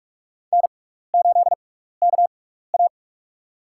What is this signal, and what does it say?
Morse code, heard as a keyed tone, about 46 wpm, spaced as for 15 wpm, 700 Hz. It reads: N8KA
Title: USA Call Signs - 4 Characters - 6x Character Spacing - 45wpm